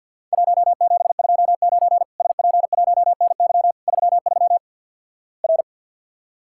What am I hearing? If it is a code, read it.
Morse code, heard as a keyed tone, about 50 wpm, 700 Hz. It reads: 1729 SP1NY 33 R
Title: Worked All Europe DX Contest - QTC Exchange with Repeats - 50wpm